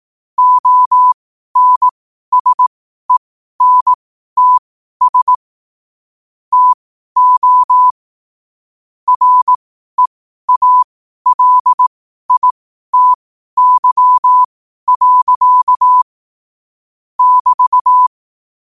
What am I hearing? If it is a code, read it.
Morse code, heard as a keyed tone, about 18 wpm, spaced as for 12 wpm, 1000 Hz. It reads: ONSENTS TO REALITY. <BT>
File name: qotdfiles / 12wpm